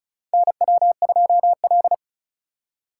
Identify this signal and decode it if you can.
Morse code, heard as a keyed tone, about 34 wpm, 700 Hz. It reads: NW2L
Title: USA Call Signs - 4 Characters - 2x Character Spacing - 35wpm